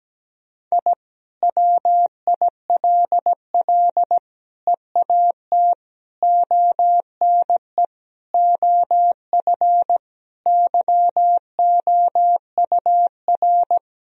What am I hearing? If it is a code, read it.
Morse code, heard as a keyed tone, about 17 wpm, 700 Hz. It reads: I WILL EAT ONE OF YOUR
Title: Sentences from Top 1000 Words - Review Part 1 - 17wpm